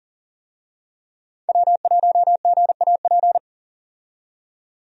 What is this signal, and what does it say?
Morse code, heard as a keyed tone, about 40 wpm, 700 Hz. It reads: W1GAP
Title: USA Call Signs - 5 Characters - 40wpm